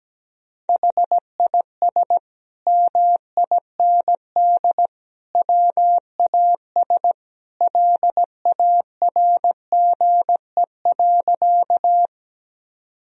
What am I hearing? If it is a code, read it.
Morse code, heard as a keyed tone, about 17 wpm, 700 Hz. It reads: HIS MIND WAS LARGE.
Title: Sentences from Top 1000 Words - Review Part 1 - 17wpm